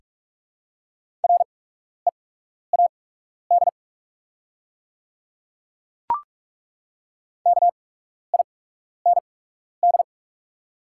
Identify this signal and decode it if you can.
Morse code, heard as a keyed tone, about 41 wpm, spaced as for 12 wpm, 690 Hz. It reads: READ KIND
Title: 8x Character Spacing - Top 200 Words - 45wpm